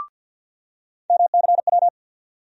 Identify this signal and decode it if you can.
Morse code, heard as a keyed tone, about 54 wpm, 700 Hz. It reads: MCW
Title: USA Call Sign Postfixes - 3 Letters 50wpm